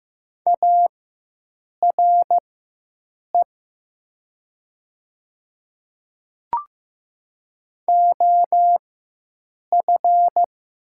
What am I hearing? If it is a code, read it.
Morse code, heard as a keyed tone, about 15 wpm, spaced as for 7 wpm, 700 Hz. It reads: ARE OF
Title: 4x Character Spacing - Top 200 Words - 15wpm